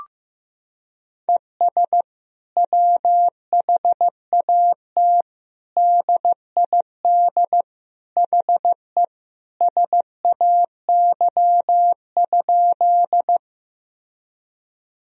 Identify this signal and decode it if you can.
Morse code, heard as a keyed tone, about 15 wpm, 700 Hz. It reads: ES WHAT DID HE SAY?
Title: Sentences from Top 200 Words and Abbreviations 15wpm